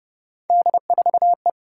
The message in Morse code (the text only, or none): D4E